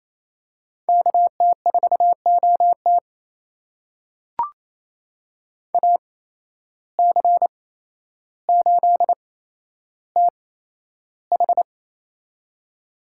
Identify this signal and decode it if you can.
Morse code, heard as a keyed tone, about 28 wpm, spaced as for 12 wpm, 700 Hz. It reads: KT4OT AC8TH
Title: USA Call Signs - 5 Characters - 8x Character Spacing - 28wpm